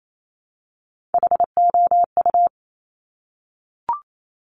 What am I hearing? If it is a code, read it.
Morse code, heard as a keyed tone, about 28 wpm, 700 Hz. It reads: HOU
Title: Top Three Letter Combinations in Words 28wpm